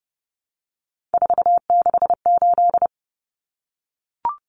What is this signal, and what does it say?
Morse code, heard as a keyed tone, about 30 wpm, 700 Hz. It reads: 468
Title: Numbers - 3-Digit 30wpm